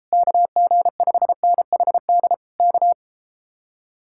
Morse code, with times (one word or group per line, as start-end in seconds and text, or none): 0.1-2.3 KG5NHD
2.6-2.9 K